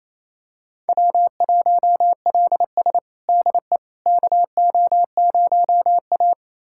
Morse code, heard as a keyed tone, about 28 wpm, 700 Hz. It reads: W1LS DE KO0A
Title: USA Callsign DE Callsign - 4 Characters - 28wpm